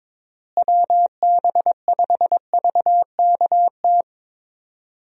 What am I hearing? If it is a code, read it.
Morse code, heard as a keyed tone, about 22 wpm, 700 Hz. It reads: WB5VKT
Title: USA Call Signs - 6 Characters - Repeated 3x - 22wpm